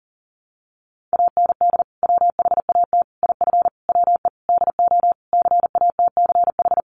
Morse code, heard as a keyed tone, about 40 wpm, 700 Hz. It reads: AND WHAT IF WE DO CATCH
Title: Sentences from Top 1500 Words 40wpm